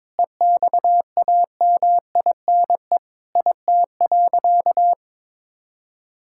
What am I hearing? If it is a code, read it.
Morse code, heard as a keyed tone, about 22 wpm, 700 Hz. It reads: EXAMINE IT.
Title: Sentences from Top 1600 Words 22wpm